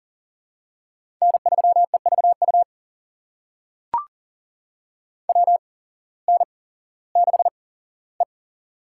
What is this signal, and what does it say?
Morse code, heard as a keyed tone, about 42 wpm, spaced as for 19 wpm, 700 Hz. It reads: N3EVU WN6E
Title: USA Call Signs - 8x Character Spacing - 40wpm